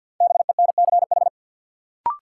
Morse code, heard as a keyed tone, about 53 wpm, 700 Hz. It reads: BENCH
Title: Sentences from Top 1400 Words - New Word Drill Alt - 50wpm